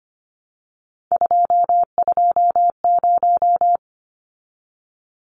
Morse code, heard as a keyed tone, about 25 wpm, 700 Hz. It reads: 220
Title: Numbers - 3-Digit 25wpm